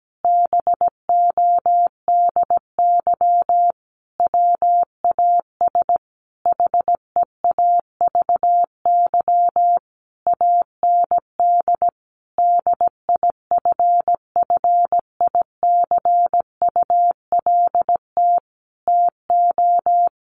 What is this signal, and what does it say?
Morse code, heard as a keyed tone, about 17 wpm, 700 Hz. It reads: BODY WAS HEAVY AND DIFFICULT TO
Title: Sentences from Top 1400 Words 17wpm